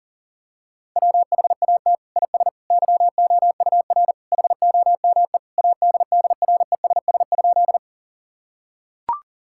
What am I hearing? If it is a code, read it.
Morse code, heard as a keyed tone, about 40 wpm, 700 Hz. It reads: WHAT IS YOUR HOME ADDRESS?
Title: Sentences from Top 1000 Words - Review Part 6 - 40wpm